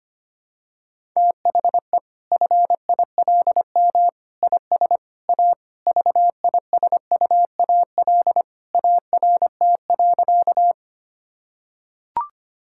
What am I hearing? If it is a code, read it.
Morse code, heard as a keyed tone, about 25 wpm, 700 Hz. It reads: THE FILM IS A VISUAL ART.
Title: Sentences from Top 1000 Words - Review Part 6 - 25wpm